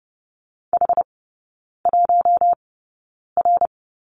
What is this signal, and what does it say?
Morse code, heard as a keyed tone, about 30 wpm, spaced as for 9 wpm, 700 Hz. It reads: H1R
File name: Lesson 19e - Letter w - Three Characters 30wpm